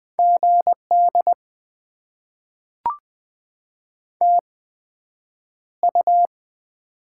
Morse code, heard as a keyed tone, about 20 wpm, spaced as for 9 wpm, 700 Hz. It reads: GD TU
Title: Top 90 QSO Elements - 8x Character Spacing - 20wpm